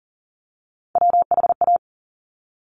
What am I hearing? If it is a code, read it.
Morse code, heard as a keyed tone, about 40 wpm, 700 Hz. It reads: WHA
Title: Top Three Letter Combinations in Words 40wpm